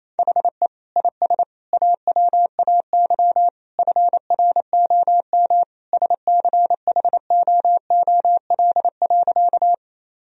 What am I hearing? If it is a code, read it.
Morse code, heard as a keyed tone, about 28 wpm, 700 Hz. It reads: HE IS AWAY FROM SCHOOL.
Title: Sentences from Top 1000 Words - Review Part 1 - 28wpm